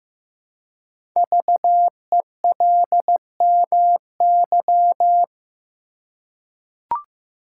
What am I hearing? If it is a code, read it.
Morse code, heard as a keyed tone, about 15 wpm, 700 Hz. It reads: VELMY
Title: Code Groups - 5 Letters - 15wpm